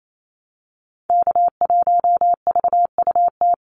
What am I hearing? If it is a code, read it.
Morse code, heard as a keyed tone, about 28 wpm, 700 Hz. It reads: K1VUT
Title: Call Signs - 28wpm